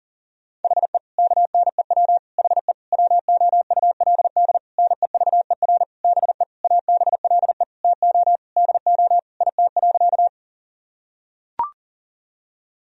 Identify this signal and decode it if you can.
Morse code, heard as a keyed tone, about 37 wpm, 700 Hz. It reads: HE KNEW HE WOULD NEVER BE ABLE TO DO IT.
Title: Sentences from Top 1000 Words - Review Part 2 - 40wpm